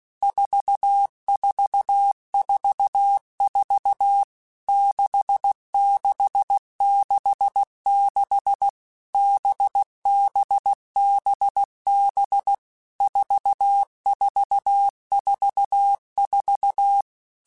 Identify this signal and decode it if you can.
Morse code, heard as a keyed tone, about 16 wpm, 800 Hz. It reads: 4444 6666 BBBB 4444
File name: Basic Letter Groups By7Groups / VB46 / VB46@15